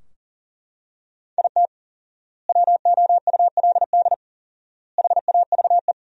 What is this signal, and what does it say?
Morse code, heard as a keyed tone, about 40 wpm, 700 Hz. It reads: IT WOULD HAVE
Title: 4x Word-Spacing - Sentences from Top 400 Words - 40wpm